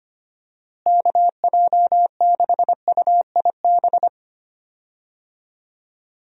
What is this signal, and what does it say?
Morse code, heard as a keyed tone, about 25 wpm, 700 Hz. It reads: KJ6UIB
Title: USA Call Signs - 6 Characters - Repeated 3x - 25wpm